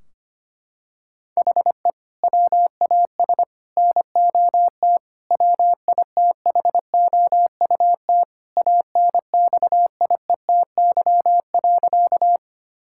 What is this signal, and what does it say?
Morse code, heard as a keyed tone, about 25 wpm, 700 Hz. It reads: HE WAS NOT WITHOUT ANXIETY.
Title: Sentences from Top 1600 Words 25wpm